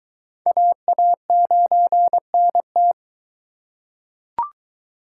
Morse code, heard as a keyed tone, about 22 wpm, 700 Hz. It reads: AA9NT